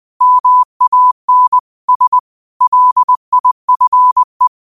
Morse code, heard as a keyed tone, about 20 wpm, 1000 Hz. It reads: MAN S LIFE